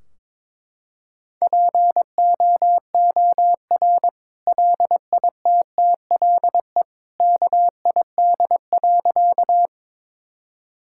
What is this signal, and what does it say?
Morse code, heard as a keyed tone, about 21 wpm, 700 Hz. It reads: POOR LITTLE KID.
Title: Sentences from Top 1600 Words 22wpm